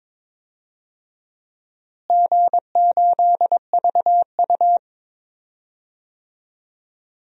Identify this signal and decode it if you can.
Morse code, heard as a keyed tone, about 21 wpm, 700 Hz. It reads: G8VU
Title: DXCC-100 Call Signs - 4 Characters - Repeated 2x - 22wpm